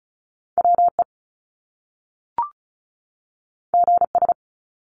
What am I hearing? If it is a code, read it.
Morse code, heard as a keyed tone, about 36 wpm, 700 Hz. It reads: WE GS
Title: Top Two Letter Combinations in Words 35wpm